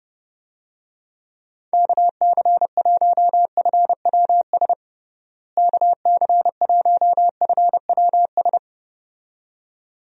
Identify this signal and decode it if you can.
Morse code, heard as a keyed tone, about 30 wpm, 700 Hz. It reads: KC1FWS KC1FWS
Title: USA Call Signs - 6 Characters - Repeated 2x - 30wpm